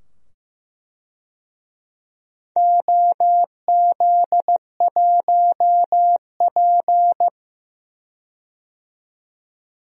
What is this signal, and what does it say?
Morse code, heard as a keyed tone, about 15 wpm, 700 Hz. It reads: OZ1P
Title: DXCC-100 Call Signs - 4 Characters - 15wpm